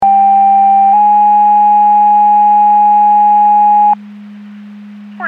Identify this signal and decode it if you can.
Two-tone page, heard as a one-second tone, then a three-second tone, about 790 then 830 Hz. It is heard from 0.0 to 3.9 s.
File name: Portage 154.385 Paging